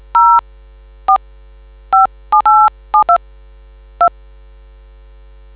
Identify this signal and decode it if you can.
DTMF, heard as telephone touch tones, tones of about 110 ms. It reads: 0 4 57872 2